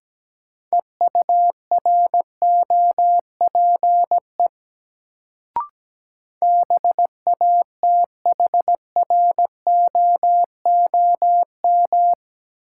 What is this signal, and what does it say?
Morse code, heard as a keyed tone, about 17 wpm, 700 Hz. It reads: EUROPE BATHROOM
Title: Sets of 5 Words Encore - New Word Drill - 17wpm